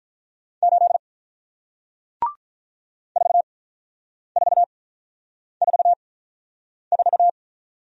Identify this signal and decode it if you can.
Morse code, heard as a keyed tone, about 48 wpm, spaced as for 9 wpm, 690 Hz. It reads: 8 4444